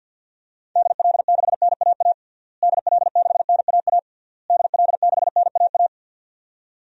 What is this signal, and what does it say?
Morse code, heard as a keyed tone, about 52 wpm, 700 Hz. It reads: DL6NAA DL6NAA DL6NAA
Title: DXCC-100 Call Signs - 6 Characters - Repeated 3x - 50wpm